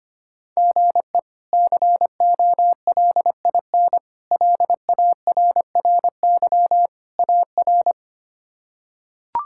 Morse code, heard as a keyed tone, about 25 wpm, 700 Hz. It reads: GE COLIN LARRY AR
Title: K1USN SST QSO Practice - 25wpm